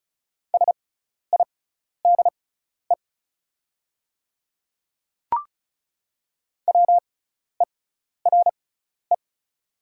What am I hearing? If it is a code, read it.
Morse code, heard as a keyed tone, about 34 wpm, spaced as for 12 wpm, 700 Hz. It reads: SIDE WERE
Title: 6x Character Spacing - Top 100 Words - 35wpm